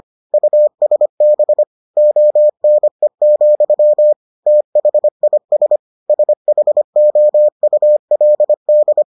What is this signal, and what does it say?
Morse code, heard as a keyed tone, about 25 wpm, 600 Hz. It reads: USB ONE, THIS SHOULD